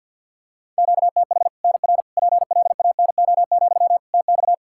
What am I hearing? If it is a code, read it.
Morse code, heard as a keyed tone, about 52 wpm, 700 Hz. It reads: QTH NR PLANO, TX
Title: Practice QSO - Exchange 1 - Beginning - 50wpm